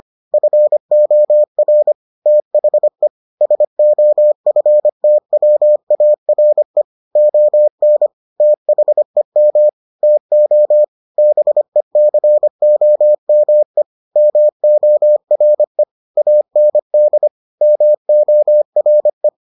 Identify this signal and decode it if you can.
Morse code, heard as a keyed tone, about 25 wpm, 600 Hz. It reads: FOR THE SOFTWARE ON THEM TO BECOME MORE AND MORE